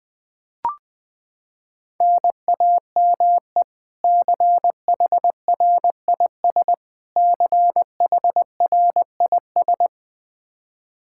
Morse code, heard as a keyed tone, about 20 wpm, 700 Hz. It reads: NAME CHRIS CHRIS
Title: Practice QSO - Exchange 1 - Beginning - 20wpm